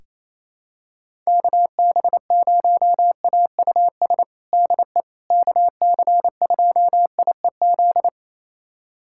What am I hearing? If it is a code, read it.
Morse code, heard as a keyed tone, about 28 wpm, 700 Hz. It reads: KB0AUS DE KC2IEZ
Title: USA Callsign DE Callsign 28wpm